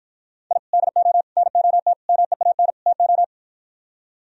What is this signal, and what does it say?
Morse code, heard as a keyed tone, about 52 wpm, 700 Hz. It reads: I DO NOT MEAN TO